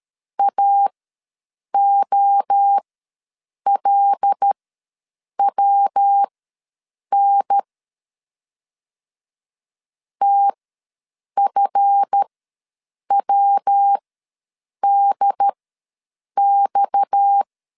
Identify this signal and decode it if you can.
Morse code, heard as a keyed tone, about 13 wpm, spaced as for 7 wpm, 800 Hz. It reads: AOLWN TFWDX
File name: LEZIONI / 7 GF9Z / 7 GF ws 12wpm 3140